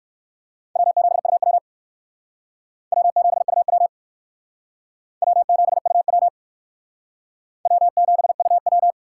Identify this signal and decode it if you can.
Morse code, heard as a keyed tone, about 61 wpm, 700 Hz. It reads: W7UW W7UW W7UW W7UW